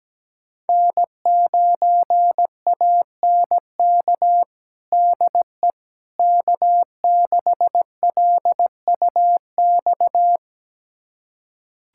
Morse code, heard as a keyed tone, about 17 wpm, 700 Hz. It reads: N9ANK DE K6LUX
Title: USA Callsign DE Callsign 17wpm